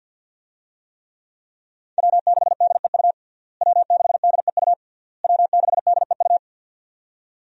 Morse code, heard as a keyed tone, about 53 wpm, 700 Hz. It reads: W6DEU W6DEU W6DEU
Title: USA Call Signs - 5 Characters - Repeated 3x - 50wpm